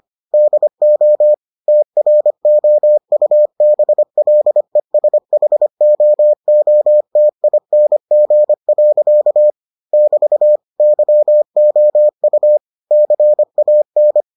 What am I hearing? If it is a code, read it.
Morse code, heard as a keyed tone, about 25 wpm, 600 Hz. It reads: DO TROUBLESHOOTING. <BT> YOU CAN